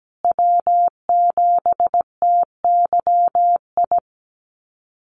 Morse code, heard as a keyed tone, about 17 wpm, 700 Hz. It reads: W7TYI
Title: USA Call Signs Repeated 2x 17wpm